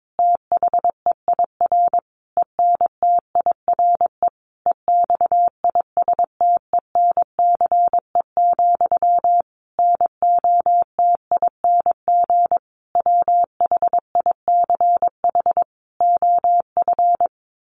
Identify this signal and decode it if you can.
Morse code, heard as a keyed tone, about 22 wpm, 700 Hz. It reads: THEIR ENTIRE EXISTENCE, NOTING WHICH OF
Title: A Sound Of Thunder 22wpm